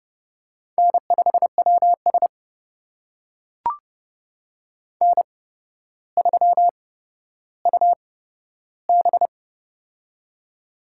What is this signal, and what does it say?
Morse code, heard as a keyed tone, about 31 wpm, spaced as for 13 wpm, 700 Hz. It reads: N5WS N3UB
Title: USA Call Signs - 4 Characters - 8x Character Spacing - 30wpm